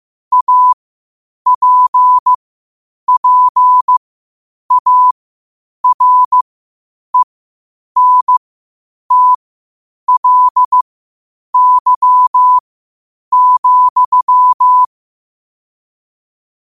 Morse code, heard as a keyed tone, about 15 wpm, spaced as for 8 wpm, 1000 Hz. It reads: APPARENTLY,
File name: qotdfiles / 8wpm